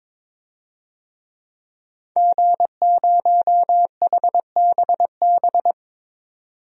Morse code, heard as a keyed tone, about 22 wpm, 700 Hz. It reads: G0HBB
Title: DXCC-100 Call Signs - 5 Characters - 22wpm